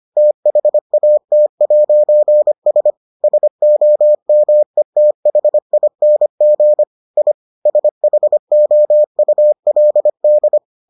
Morse code, heard as a keyed tone, about 25 wpm, 600 Hz. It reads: THAT'S SOMETHING I SHOULD